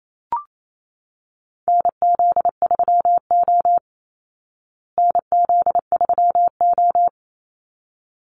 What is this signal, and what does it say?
Morse code, heard as a keyed tone, about 28 wpm, 700 Hz. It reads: NZ3O NZ3O
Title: Call Signs Repeated 3x - 28wpm